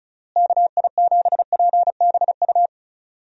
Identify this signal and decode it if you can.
Morse code, heard as a keyed tone, about 36 wpm, 700 Hz. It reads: KI7PBU
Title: USA Call Signs - 6 Characters - Repeated 3x - 35wpm